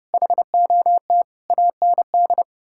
Morse code, heard as a keyed tone, about 40 wpm, 700 Hz. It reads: HOT AND